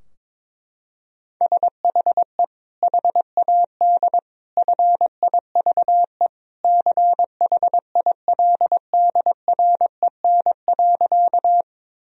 Morse code, heard as a keyed tone, about 22 wpm, 700 Hz. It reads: SHE HAD FIVE CHILDREN.
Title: Sentences from Top 1000 Words - Review Part 2 - 22wpm